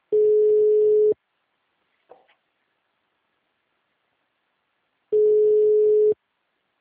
Ringback tone (Europe, 425 Hz).